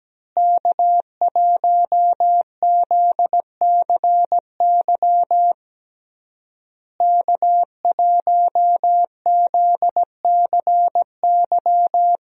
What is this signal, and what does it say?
Morse code, heard as a keyed tone, about 17 wpm, 700 Hz. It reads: K1ZCY K1ZCY
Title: USA Call Signs - 5 Characters - Repeated 3x - 17wpm